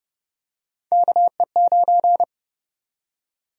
Morse code, heard as a keyed tone, about 30 wpm, 700 Hz. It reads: KE9
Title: USA Call Sign Prefixes - Repeated 2x - 30wpm